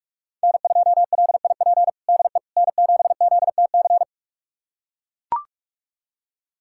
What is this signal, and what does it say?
Morse code, heard as a keyed tone, about 41 wpm, 700 Hz. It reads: N2LIP DE N7ZTC
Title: USA Callsign DE Callsign - 5 Characters - 45wpm